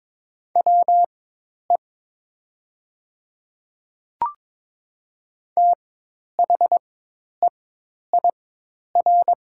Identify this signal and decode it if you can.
Morse code, heard as a keyed tone, about 22 wpm, spaced as for 10 wpm, 690 Hz. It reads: WE THEIR
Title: 4x Character Spacing - Top 200 Words - 22wpm